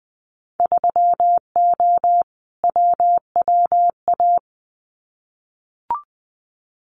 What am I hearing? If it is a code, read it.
Morse code, heard as a keyed tone, about 20 wpm, 700 Hz. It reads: 3O WWA
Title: Winter Field Day Exchanges Encore 20wpm